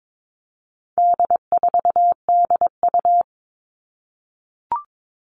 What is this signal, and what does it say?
Morse code, heard as a keyed tone, about 22 wpm, 700 Hz. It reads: D4DU
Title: All Cut Numbers - 4-Digit 22wpm